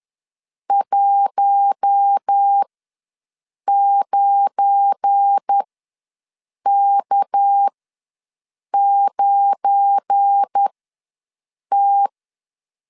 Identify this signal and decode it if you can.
Morse code, heard as a keyed tone, about 10 wpm, spaced as for 6 wpm, 800 Hz. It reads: 19K9T
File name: LEZIONI / 7 GF9Z / 7 GF9 ws 10wpm 4034